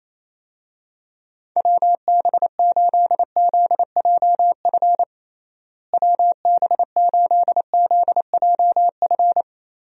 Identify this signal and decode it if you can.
Morse code, heard as a keyed tone, about 28 wpm, 700 Hz. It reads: WB8ZJF WB8ZJF
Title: USA Call Signs - 6 Characters - Repeated 3x - 28wpm